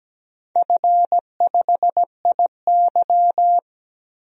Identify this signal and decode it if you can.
Morse code, heard as a keyed tone, about 17 wpm, 700 Hz. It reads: F5IY
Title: DXCC-100 Call Signs - 4 Characters - 17wpm